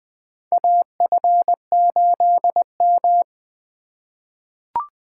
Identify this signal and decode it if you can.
Morse code, heard as a keyed tone, about 20 wpm, 700 Hz. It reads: AF8M
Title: USA Call Signs - 4 Characters - 2x Character Spacing - 20wpm